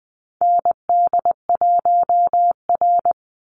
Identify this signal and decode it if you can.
Morse code, heard as a keyed tone, about 20 wpm, 700 Hz. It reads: ND1R
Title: Call Signs - 20wpm